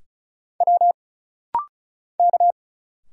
Morse code, heard as a keyed tone, about 34 wpm, 700 Hz. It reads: W K
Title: Single Letters - Rapid-Fire 35wpm